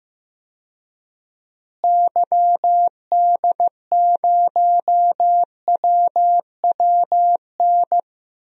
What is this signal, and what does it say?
Morse code, heard as a keyed tone, about 15 wpm, 700 Hz. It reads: YD0WWN
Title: DXCC-100 Call Signs - 6 Characters - Repeated 3x - 15wpm